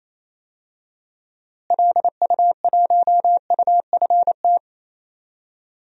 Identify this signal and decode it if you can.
Morse code, heard as a keyed tone, about 28 wpm, 700 Hz. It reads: LU1UFT
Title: DXCC-100 Call Signs - 6 Characters - Repeated 2x - 28wpm